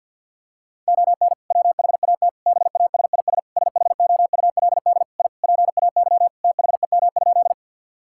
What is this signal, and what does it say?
Morse code, heard as a keyed tone, about 52 wpm, 690 Hz. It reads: ON WHAT BASIS SHOULD I PAY THEM?
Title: Sentences from Top 1000 Words - Review Part 5 - 50wpm